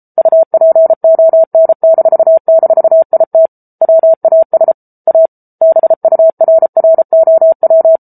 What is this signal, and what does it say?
Morse code, heard as a keyed tone, about 34 wpm, 650 Hz. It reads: UPON--IT WAS A BURROW
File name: Book Two / 07-Chapter 7 - 33 WPM